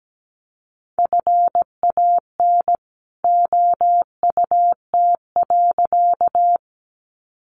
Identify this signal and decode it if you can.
Morse code, heard as a keyed tone, about 17 wpm, 700 Hz. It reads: FAN OUT.
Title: Sentences from Top 1300 Words 17wpm